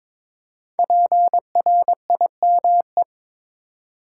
Morse code, heard as a keyed tone, about 22 wpm, 700 Hz. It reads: PRIME
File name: practice / 22wpm / 5 Letter Words - First 3 Letters Given - 22wpm